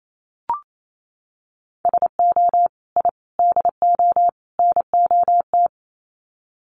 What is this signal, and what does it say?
Morse code, heard as a keyed tone, about 28 wpm, 700 Hz. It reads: SO I DO NOT
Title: Sets of 4 Words 28wpm